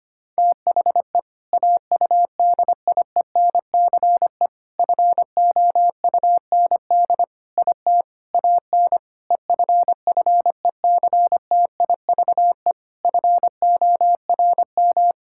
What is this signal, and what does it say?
Morse code, heard as a keyed tone, about 25 wpm, 700 Hz. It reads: THE AUDIENCE FOUND IT AN EFFECTIVE FORM
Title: Sentences from Top 1000 Words - Review Part 6 - 25wpm